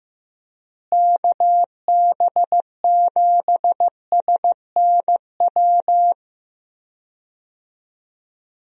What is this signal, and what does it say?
Morse code, heard as a keyed tone, about 15 wpm, 700 Hz. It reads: KB7SNW
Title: USA Call Signs - 6 Characters - Repeated 3x - 15wpm